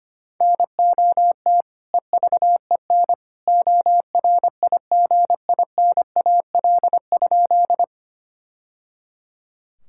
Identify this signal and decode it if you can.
Morse code, heard as a keyed tone, about 25 wpm, 700 Hz. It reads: NOT EVEN ORIGINAL?
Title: Sentences from Top 1000 Words - Review Part 6 - 25wpm